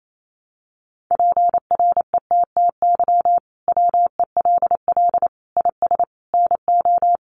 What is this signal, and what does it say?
Morse code, heard as a keyed tone, about 28 wpm, 700 Hz. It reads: PRETTY WELL IS NO
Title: Sentences from Top 1000 Words 28wpm